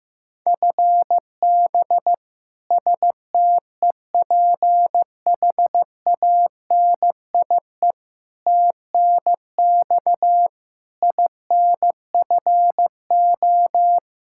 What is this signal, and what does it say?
Morse code, heard as a keyed tone, about 15 wpm, 700 Hz. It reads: FB STEPHANIE TNX INFO
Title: Practice QSO - Exchange 3 - More Info - 15wpm